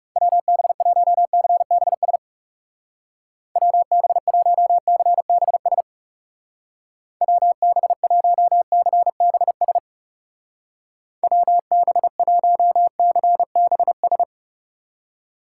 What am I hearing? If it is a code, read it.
Morse code, heard as a keyed tone, about 36 wpm, 700 Hz. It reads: WB1CBS WB1CBS WB1CBS WB1CBS
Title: Speed Racing - USA Call Signs 30wpm